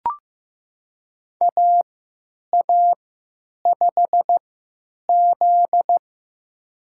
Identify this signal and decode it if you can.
Morse code, heard as a keyed tone, about 15 wpm, spaced as for 8 wpm, 700 Hz. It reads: AA5Z
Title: USA Call Signs - 4 Characters - 3x Character Spacing - 15wpm